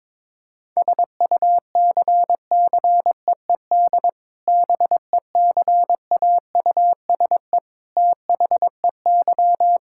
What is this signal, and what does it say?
Morse code, heard as a keyed tone, about 22 wpm, 700 Hz. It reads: SUCCEED BECAUSE THEY
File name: practice / 22wpm / Sentences with 2-Word Phrases - 22wpm